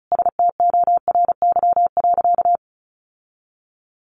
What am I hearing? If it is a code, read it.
Morse code, heard as a keyed tone, about 34 wpm, 700 Hz. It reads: STORY.
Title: Sentences from Top 1100 Words 35wpm